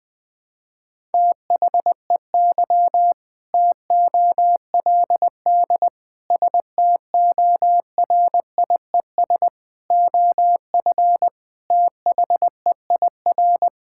Morse code, heard as a keyed tone, about 19 wpm, 700 Hz. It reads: THEY TOLD STORIES OF THEIR
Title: Sentences with 2-Word Phrases - Spoken Beginning - 20wpm